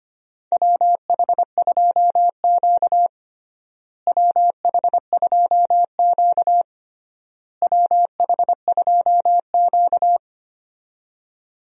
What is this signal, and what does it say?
Morse code, heard as a keyed tone, about 25 wpm, 700 Hz. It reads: WH2Q WH2Q WH2Q
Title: USA Call Signs - 4 Characters - Repeated 3x - 25wpm